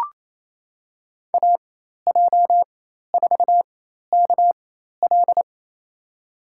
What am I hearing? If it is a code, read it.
Morse code, heard as a keyed tone, about 27 wpm, spaced as for 13 wpm, 700 Hz. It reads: AJ4KL
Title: USA Call Signs - 5 Characters - 4x Character Spacing - 28wpm